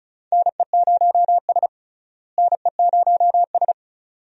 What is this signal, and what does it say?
Morse code, heard as a keyed tone, about 34 wpm, 700 Hz. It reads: NE0S NE0S
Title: USA Call Signs - 4 Characters - Repeated 2x - 35wpm